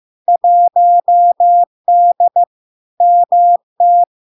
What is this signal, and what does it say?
Morse code, heard as a keyed tone, about 15 wpm, 700 Hz. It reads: 1D MT
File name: LARK-FD-2022-Log-in-CW-15wpm-700-Hz